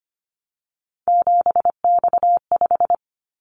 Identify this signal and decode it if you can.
Morse code, heard as a keyed tone, about 25 wpm, 700 Hz. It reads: 7X5
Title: Callsign Prefixes for Top 100 DXCC - 25wpm